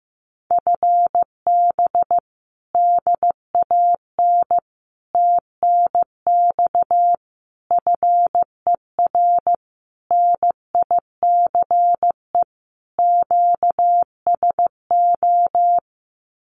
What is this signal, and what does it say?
Morse code, heard as a keyed tone, about 15 wpm, 700 Hz. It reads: FB DAN TNX FER NICE QSO
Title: Common QSO Phrases Encore 15wpm